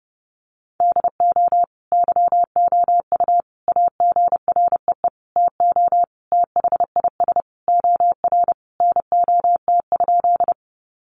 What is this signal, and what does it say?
Morse code, heard as a keyed tone, about 30 wpm, 700 Hz. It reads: DO YOU AGREE TO THIS OR NOT?
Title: Sentences from Top 1100 Words 30wpm